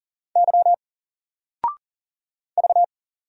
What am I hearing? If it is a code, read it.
Morse code, heard as a keyed tone, about 40 wpm, 700 Hz. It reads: Y V
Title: Single Letter-Number 40wpm